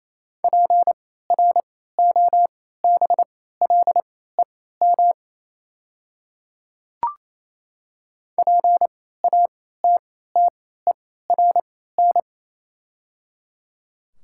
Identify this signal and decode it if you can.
Morse code, heard as a keyed tone, about 28 wpm, spaced as for 16 wpm, 700 Hz. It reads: PROBLEM PATTERN
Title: 3x Character Spacing - Top 500 Words - 28wpm